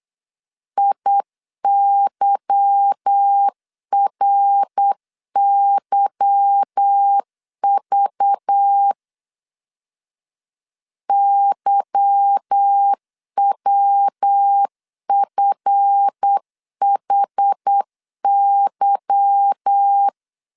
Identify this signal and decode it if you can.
Morse code, heard as a keyed tone, about 8 wpm, 800 Hz. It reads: IYRYV YWFHY
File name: LEZIONI / 8 YJ45 / 8 Y 8wpm 3245